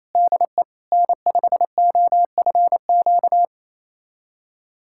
Morse code, heard as a keyed tone, about 28 wpm, 700 Hz. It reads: DE N5OFQ
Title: USA Callsign DE Callsign - 5 Characters - 28wpm